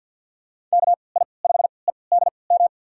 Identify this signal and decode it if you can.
Morse code, heard as a keyed tone, about 50 wpm, spaced as for 28 wpm, 700 Hz. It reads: KI5EDM
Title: USA Call Signs - 6 Characters - 3x Character Spacing - 50wpm